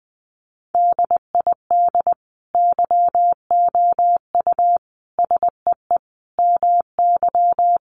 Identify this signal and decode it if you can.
Morse code, heard as a keyed tone, about 20 wpm, 700 Hz. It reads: DID YOU SEE MY
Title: Sentences from Top 1500 Words 20wpm